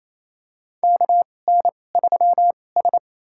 Morse code, heard as a keyed tone, about 27 wpm, spaced as for 20 wpm, 700 Hz. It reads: KN3S
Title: USA Call Signs - 2x Character Spacing - 28wpm